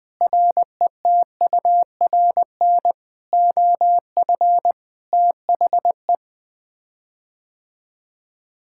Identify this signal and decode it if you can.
Morse code, heard as a keyed tone, about 20 wpm, 700 Hz. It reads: RETURN OF THE
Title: Sets of 9 Words - 20wpm